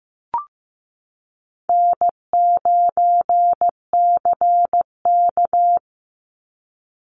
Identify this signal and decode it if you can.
Morse code, heard as a keyed tone, about 15 wpm, 700 Hz. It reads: N9CK
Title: Call Signs Repeated 3x - 15wpm